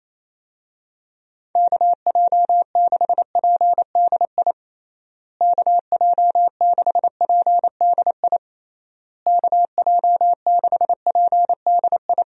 Morse code, heard as a keyed tone, about 28 wpm, 700 Hz. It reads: KJ6PDI KJ6PDI KJ6PDI
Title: USA Call Signs - 6 Characters - Repeated 3x - 28wpm